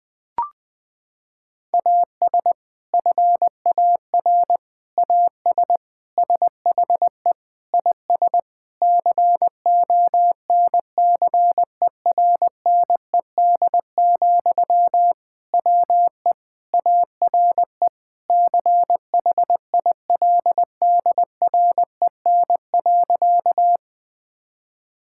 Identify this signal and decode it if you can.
Morse code, heard as a keyed tone, about 21 wpm, 700 Hz. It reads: AS FAR AS SHE IS CONCERNED, WE ARE CHILDREN.
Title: Sentences from Top 1000 Words - Review Part 6 - 20wpm